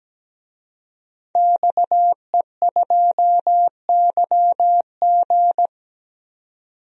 Morse code, heard as a keyed tone, about 17 wpm, 700 Hz. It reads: XE2YG
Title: DXCC-100 Call Signs - 5 Characters - 17wpm